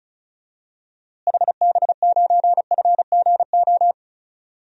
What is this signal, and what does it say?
Morse code, heard as a keyed tone, about 34 wpm, 700 Hz. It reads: HB9FGO
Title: DXCC-100 Call Signs - 6 Characters - Repeated 2x - 35wpm